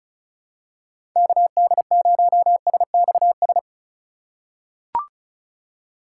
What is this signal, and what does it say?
Morse code, heard as a keyed tone, about 36 wpm, 700 Hz. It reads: KD0SXS
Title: USA Call Signs - 6 Characters - 4x Character Spacing - 35wpm